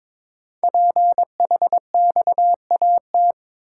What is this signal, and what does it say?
Morse code, heard as a keyed tone, about 22 wpm, 700 Hz. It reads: PHXAT
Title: Code Groups - 5 Letter:Numbers - 22wpm